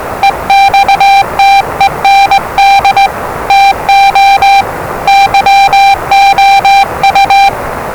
Morse code, heard as a keyed tone, about 20 wpm, 790 Hz, with noise. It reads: EXTEND TO YOU
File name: WellingtonRadioZlwCloseDownOnHfAndMf / 01-Zlw 4277khz 417.5 qrt until 0925z